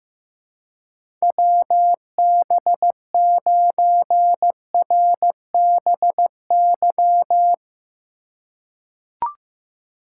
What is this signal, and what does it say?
Morse code, heard as a keyed tone, about 15 wpm, 700 Hz. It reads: WB9RBY
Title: USA Call Signs - 6 Characters - 4x Character Spacing - 15wpm